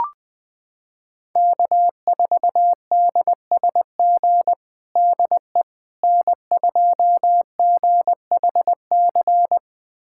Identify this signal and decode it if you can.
Morse code, heard as a keyed tone, about 21 wpm, 700 Hz. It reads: K4DSG DE N2GHC
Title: USA Callsign DE Callsign - 5 Characters - 20wpm